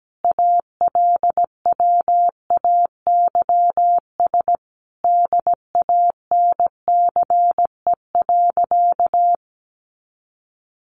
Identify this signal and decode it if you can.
Morse code, heard as a keyed tone, about 17 wpm, 700 Hz. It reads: ALWAYS DANCE.
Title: Sentences from Top 700 Words 17wpm